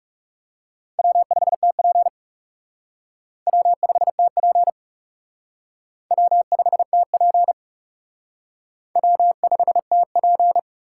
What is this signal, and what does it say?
Morse code, heard as a keyed tone, about 37 wpm, 700 Hz. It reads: W5TP W5TP W5TP W5TP
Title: Speed Racing - USA Call Signs 30wpm